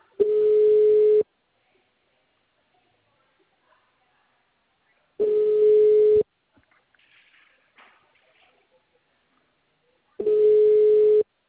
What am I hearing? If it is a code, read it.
Ringback tone (Europe, 425 Hz).